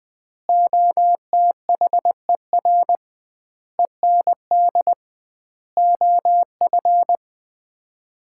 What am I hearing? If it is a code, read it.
Morse code, heard as a keyed tone, about 20 wpm, 700 Hz. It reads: OTHER END OF
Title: 2x Word-Spacing - Sets of 5 Words - 20wpm